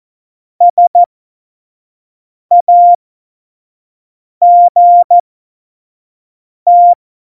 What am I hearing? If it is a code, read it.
Morse code, heard as a keyed tone, about 14 wpm, spaced as for 5 wpm, 700 Hz. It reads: SAGT
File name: Morsekurs / Witze / Witz 10.Juli 2018, 25 F70